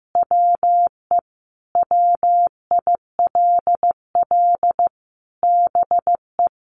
Morse code, heard as a keyed tone, about 14 wpm, 700 Hz. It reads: WE WILL BE
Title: Sentences from Top 1500 Words 15wpm